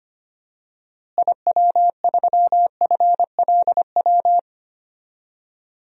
Morse code, heard as a keyed tone, about 25 wpm, 700 Hz. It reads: IW3FLW
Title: DXCC-100 Call Signs - 6 Characters - Repeated 3x - 25wpm